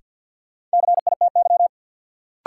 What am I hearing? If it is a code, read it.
Morse code, heard as a keyed tone, about 53 wpm, 700 Hz. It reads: CITY
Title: Sentences from Top 300 Words - New Word Drill - 50wpm